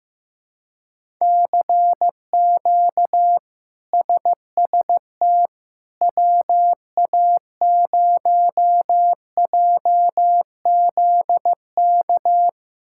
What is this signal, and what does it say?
Morse code, heard as a keyed tone, about 15 wpm, 700 Hz. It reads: CQ SST WA0JZK
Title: K1USN SST QSO Practice - 15wpm